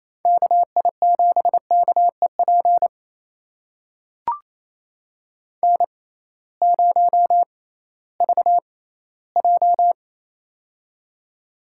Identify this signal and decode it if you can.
Morse code, heard as a keyed tone, about 28 wpm, spaced as for 16 wpm, 700 Hz. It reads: KI7KEP N0VJ
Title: USA Call Signs - 6x Character Spacing - 28wpm